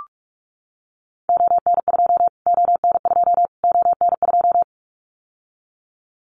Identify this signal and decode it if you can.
Morse code, heard as a keyed tone, about 44 wpm, 700 Hz. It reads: ON2 ON2 ON2
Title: Callsign Prefixes for Top 25 DXCC - Repeated 3x - 45wpm